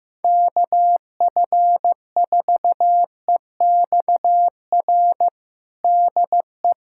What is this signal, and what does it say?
Morse code, heard as a keyed tone, about 15 wpm, 700 Hz. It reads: KF4EXR DE
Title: USA Callsign DE Callsign 15wpm